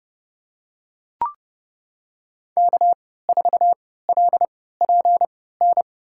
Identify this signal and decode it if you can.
Morse code, heard as a keyed tone, about 30 wpm, spaced as for 17 wpm, 700 Hz. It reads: K4LPN
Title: USA Call Signs - 5 Characters - 3x Character Spacing - 30wpm